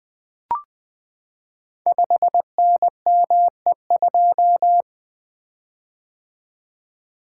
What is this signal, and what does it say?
Morse code, heard as a keyed tone, about 20 wpm, 700 Hz. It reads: 5NME2
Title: Code Groups - 5 Letter:Numbers - 20wpm